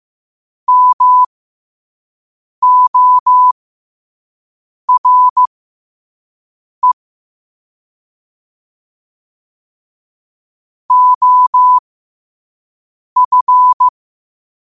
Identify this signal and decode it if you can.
Morse code, heard as a keyed tone, about 15 wpm, spaced as for 5 wpm, 1000 Hz. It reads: MORE OF
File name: qotdfiles / 5wpm